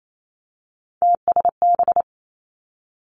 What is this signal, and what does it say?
Morse code, heard as a keyed tone, about 28 wpm, 700 Hz. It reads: TSB
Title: Three Challenging Letter-Number Combinations 28wpm